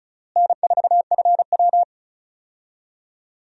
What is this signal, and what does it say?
Morse code, heard as a keyed tone, about 34 wpm, 700 Hz. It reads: N4FW
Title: USA Call Signs - 4 Characters - Repeated 3x - 35wpm